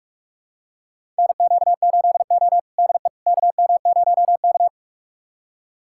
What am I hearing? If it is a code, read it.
Morse code, heard as a keyed tone, about 41 wpm, 700 Hz. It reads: NQ8O DE KM0K
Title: USA Callsign DE Callsign - 4 Characters - 45wpm